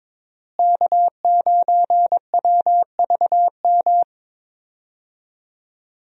Morse code, heard as a keyed tone, about 22 wpm, 700 Hz. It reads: K9WVM
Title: USA Call Signs - 5 Characters - Repeated 3x - 22wpm